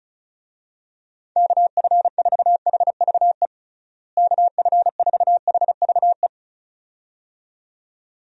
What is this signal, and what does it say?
Morse code, heard as a keyed tone, about 36 wpm, 700 Hz. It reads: KF4HVE KF4HVE
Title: USA Call Signs - 6 Characters - Repeated 2x - 35wpm